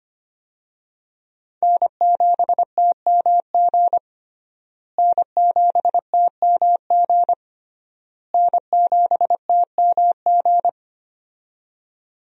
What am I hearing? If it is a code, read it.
Morse code, heard as a keyed tone, about 25 wpm, 700 Hz. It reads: N7TMG N7TMG N7TMG
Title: USA Call Signs - 5 Characters - Repeated 3x - 25wpm